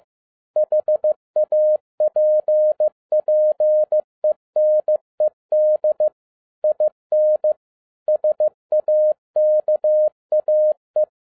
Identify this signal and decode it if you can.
Morse code, heard as a keyed tone, about 15 wpm, 610 Hz. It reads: HAPPENED IN SAKAE
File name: FIELD DAY CONTEST QRV FROM SAKAE VILLAGE 15wpm